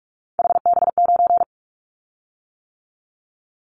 Morse code, heard as a keyed tone, about 46 wpm, 700 Hz. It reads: HB9
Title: Callsign Prefixes for Top 25 DXCC - 45wpm